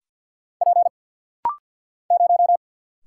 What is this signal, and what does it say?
Morse code, heard as a keyed tone, about 50 wpm, 700 Hz. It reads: P 0
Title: Single Letter-Number Alt - Rapid-Fire 50wpm